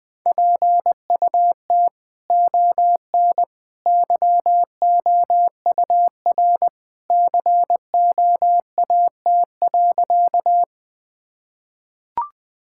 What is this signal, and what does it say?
Morse code, heard as a keyed tone, about 21 wpm, 700 Hz. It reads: PUT ON YOUR COAT.
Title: Sentences from Top 1600 Words 20wpm